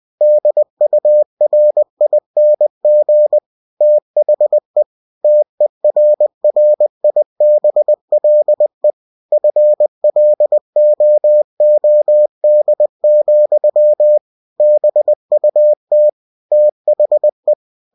Morse code, heard as a keyed tone, about 20 wpm, 600 Hz. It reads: DURING THE TERRIBLE FLOOD, BUT THE
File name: cw-books / winnie-the-pooh / winnie-20wpm-page-0100